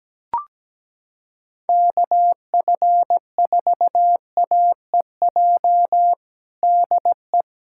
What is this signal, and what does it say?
Morse code, heard as a keyed tone, about 17 wpm, 700 Hz. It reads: KF4AEJ DE
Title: USA Callsign DE Callsign - 6 Characters - 17wpm